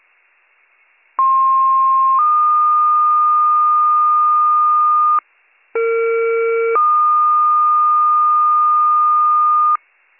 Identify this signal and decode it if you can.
Two-tone page, heard as a one-second tone, then a three-second tone, about 1060 then 1220 Hz, about 460 then 1220 Hz. It is heard from 1.2 to 9.8 s.